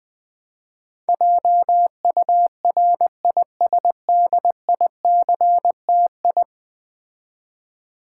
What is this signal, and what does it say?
Morse code, heard as a keyed tone, about 20 wpm, 700 Hz. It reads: JURISDICTI
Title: Build A Word - 5 Letters At A Time - 12-Letter Words - 20wpm